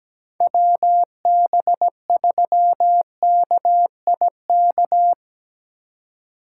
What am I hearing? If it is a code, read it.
Morse code, heard as a keyed tone, about 17 wpm, 700 Hz. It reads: WB3KIK
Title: USA Call Signs - 6 Characters - 4x Character Spacing - 17wpm